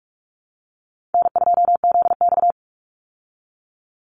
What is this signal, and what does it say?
Morse code, heard as a keyed tone, about 44 wpm, 700 Hz. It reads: N2ZX
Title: Call Signs Repeated 2x - 45wpm